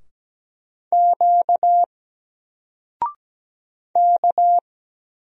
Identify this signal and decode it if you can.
Morse code, heard as a keyed tone, about 17 wpm, 700 Hz. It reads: Q K
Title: Single Letter-Number Alt 17wpm